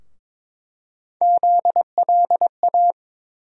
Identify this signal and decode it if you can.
Morse code, heard as a keyed tone, about 22 wpm, 700 Hz. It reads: ZLA
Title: USA Call Sign Postfixes - 3 Letters 22wpm